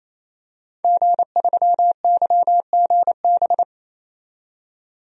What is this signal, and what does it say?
Morse code, heard as a keyed tone, about 28 wpm, 700 Hz. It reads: G3YGB
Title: DXCC-100 Call Signs - 5 Characters - 28wpm